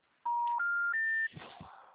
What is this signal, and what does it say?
Special information tone.
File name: out- -2004- - - .126